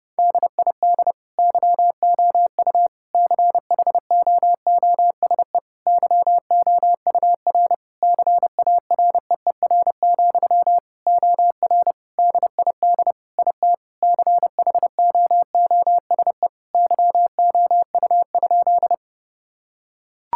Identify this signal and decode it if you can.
Morse code, heard as a keyed tone, about 30 wpm, 700 Hz. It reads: DID YOU CHOOSE YOUR CAREER, OR DID IT CHOOSE YOU?
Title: Sentences from Top 1000 Words - Review Part 6 - 30wpm